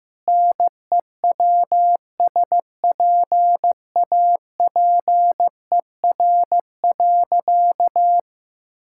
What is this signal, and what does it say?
Morse code, heard as a keyed tone, about 16 wpm, 700 Hz. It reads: NEWSPAPER.
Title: Sentences from Top 1000 Words - Review Part 5 - 15wpm